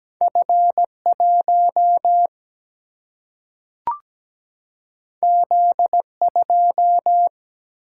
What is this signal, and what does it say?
Morse code, heard as a keyed tone, about 17 wpm, 700 Hz. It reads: F1 Z2
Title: Single-Letter Single-Number 17wpm